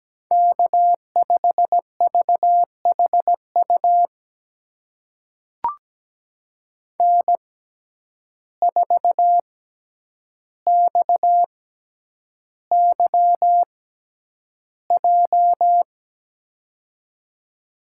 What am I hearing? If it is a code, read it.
Morse code, heard as a keyed tone, about 17 wpm, spaced as for 9 wpm, 700 Hz. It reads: K5VHU N4XYJ
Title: USA Call Signs - 5 Characters - 6x Character Spacing - 17wpm